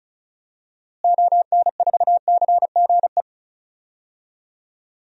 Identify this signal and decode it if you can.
Morse code, heard as a keyed tone, about 34 wpm, 700 Hz. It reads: ON4CGE
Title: DXCC-100 Call Signs - 6 Characters - Repeated 2x - 35wpm